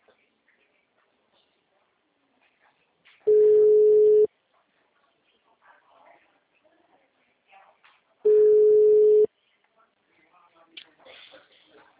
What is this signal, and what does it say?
Ringback tone (Europe, 425 Hz), with speech.
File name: out- -2013- - - .308